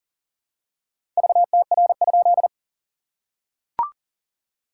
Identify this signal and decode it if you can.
Morse code, heard as a keyed tone, about 40 wpm, 690 Hz. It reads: VTR?
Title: SOTA QSO Practice 40wpm